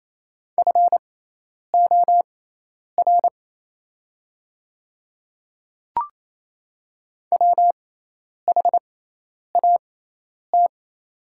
Morse code, heard as a keyed tone, about 28 wpm, spaced as for 10 wpm, 700 Hz. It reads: FOR WHAT
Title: 6x Character Spacing - Top 200 Words - 28wpm